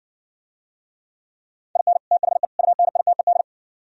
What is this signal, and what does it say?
Morse code, heard as a keyed tone, about 60 wpm, 700 Hz. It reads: IN THE UNITED